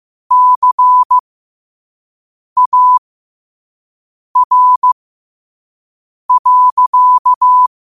Morse code, heard as a keyed tone, about 15 wpm, spaced as for 5 wpm, 1000 Hz. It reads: CAR.
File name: qotdfiles / 5wpm